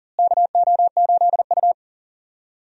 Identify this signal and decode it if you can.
Morse code, heard as a keyed tone, about 40 wpm, 700 Hz. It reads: KO8U
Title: USA Call Signs - 4 Characters - Repeated 3x - 40wpm